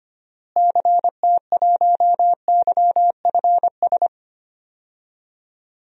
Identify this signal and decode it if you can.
Morse code, heard as a keyed tone, about 25 wpm, 700 Hz. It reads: CT1YFS
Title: DXCC-100 Call Signs - 6 Characters - Repeated 3x - 25wpm